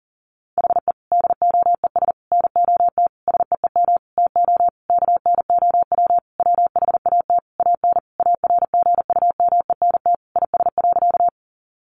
Morse code, heard as a keyed tone, about 37 wpm, 700 Hz. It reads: HE DOES NOT SEEM TO KNOW WHAT AN ARGUMENT IS.
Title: Sentences from Top 1400 Words 40wpm